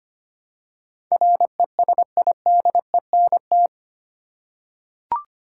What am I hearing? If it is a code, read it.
Morse code, heard as a keyed tone, about 25 wpm, 700 Hz. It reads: RESIDENT
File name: Eight Letter Words - 25wpm